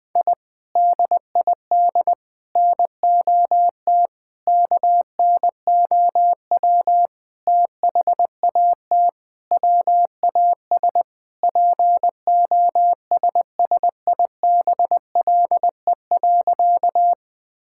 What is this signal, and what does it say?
Morse code, heard as a keyed tone, about 21 wpm, 700 Hz. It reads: I DID NOT KNOW THAT WAS POSSIBLE.
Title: Sentences with 5-Word Phrases - 20wpm